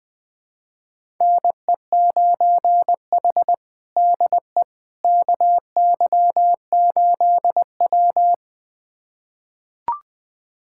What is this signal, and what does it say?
Morse code, heard as a keyed tone, about 21 wpm, 700 Hz. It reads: NE9H DE KY8W
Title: USA Callsign DE Callsign - 4 Characters - 20wpm